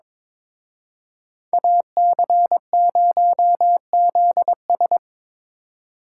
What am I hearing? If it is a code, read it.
Morse code, heard as a keyed tone, about 25 wpm, 700 Hz. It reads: AC0ZS
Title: Speed Racing - USA 2x2 Call Signs 22wpm